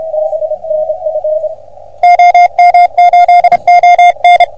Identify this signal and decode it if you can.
Morse code, heard as a keyed tone, about 31 wpm, 710 Hz, with radio noise. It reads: OM8ON
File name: OM3PA PILUP CUC 388